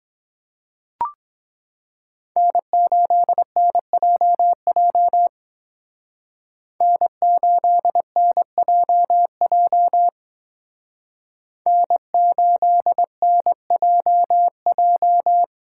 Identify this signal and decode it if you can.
Morse code, heard as a keyed tone, about 22 wpm, 700 Hz. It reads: N8NJJ N8NJJ N8NJJ